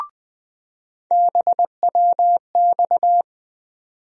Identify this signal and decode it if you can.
Morse code, heard as a keyed tone, about 21 wpm, 700 Hz. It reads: BWX
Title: USA Call Sign Postfixes - 3 Letters 20wpm